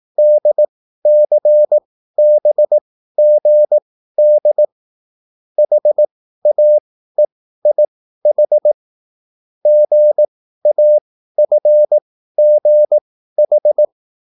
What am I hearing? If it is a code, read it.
Morse code, heard as a keyed tone, about 18 wpm, spaced as for 13 wpm, 600 Hz. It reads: DCBGD HAEIH GAFGH